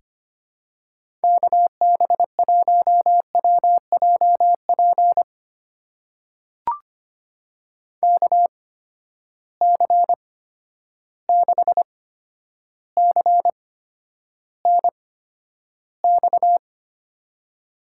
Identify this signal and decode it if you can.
Morse code, heard as a keyed tone, about 25 wpm, spaced as for 11 wpm, 700 Hz. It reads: KB1WJP KC6CNX
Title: USA Call Signs - 6 Characters - 8x Character Spacing - 25wpm